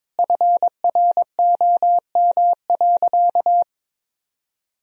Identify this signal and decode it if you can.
Morse code, heard as a keyed tone, about 21 wpm, 700 Hz. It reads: FROM.
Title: Sentences from Top 1000 Words - Review Part 4 - 22wpm